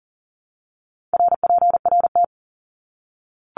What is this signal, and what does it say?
Morse code, heard as a keyed tone, about 40 wpm, 700 Hz. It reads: RPRT
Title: Abbreviations 40wpm